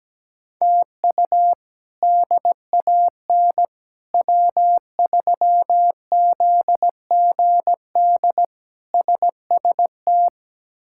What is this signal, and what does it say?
Morse code, heard as a keyed tone, about 17 wpm, 700 Hz. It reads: TU DAN W3ZGD SST
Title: K1USN SST QSO Practice - 17wpm